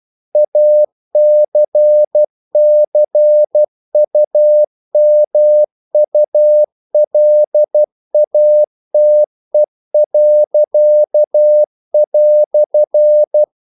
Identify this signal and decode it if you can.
Morse code, heard as a keyed tone, about 12 wpm, 600 Hz. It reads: ACCUMULATE."
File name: CW / Practice / EnglishQuotations0312